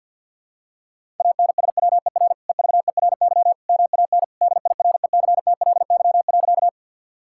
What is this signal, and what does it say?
Morse code, heard as a keyed tone, about 52 wpm, 690 Hz. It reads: ANSWER EVERY MAN DIRECTLY.
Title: Sentences from Top 1000 Words - Review Part 5 - 50wpm